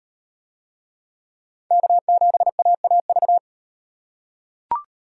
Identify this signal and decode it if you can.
Morse code, heard as a keyed tone, about 33 wpm, 700 Hz. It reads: K7AAV